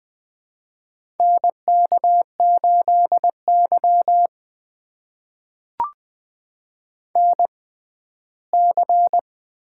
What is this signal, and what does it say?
Morse code, heard as a keyed tone, about 20 wpm, spaced as for 14 wpm, 700 Hz. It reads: NK8Y NC
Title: USA Call Signs - 4 Characters - 6x Character Spacing - 20wpm